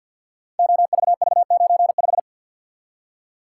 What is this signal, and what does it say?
Morse code, heard as a keyed tone, about 47 wpm, 700 Hz. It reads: OVV95
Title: Code Groups - 5 Letter:Numbers - 50wpm